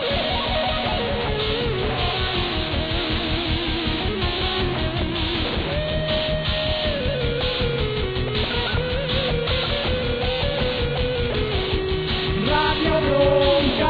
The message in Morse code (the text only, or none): none